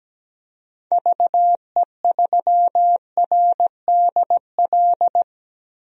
Morse code, heard as a keyed tone, about 17 wpm, 700 Hz. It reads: VE3RDL
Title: DXCC-100 Call Signs - 6 Characters - 17wpm